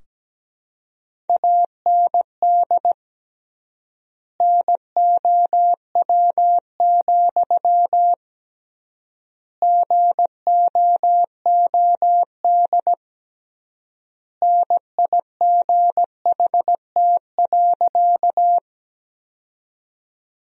Morse code, heard as a keyed tone, about 17 wpm, 700 Hz. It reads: AND NOW, GOOD NIGHT.
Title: 3x Word-Spacing - Sentences from Top 300 Words - 17wpm